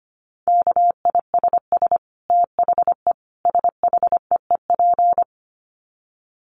Morse code, heard as a keyed tone, about 25 wpm, 700 Hz. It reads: KISS THE SHEEP
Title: Dit Challenge 25wpm